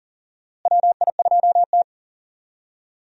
Morse code, heard as a keyed tone, about 40 wpm, 700 Hz. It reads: WI2T
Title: USA Call Signs - 4 Characters - Repeated 3x - 40wpm